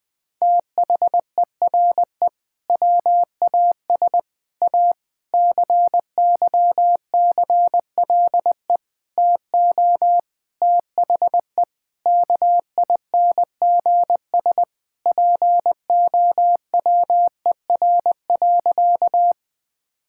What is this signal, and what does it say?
Morse code, heard as a keyed tone, about 20 wpm, 700 Hz. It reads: THERE WAS A CYCLE TO THE KINGS POWER.
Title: Sentences from Top 1000 Words - Review Part 6 - 20wpm